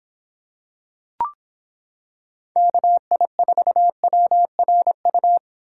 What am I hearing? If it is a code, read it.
Morse code, heard as a keyed tone, about 25 wpm, 700 Hz. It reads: KI4WRU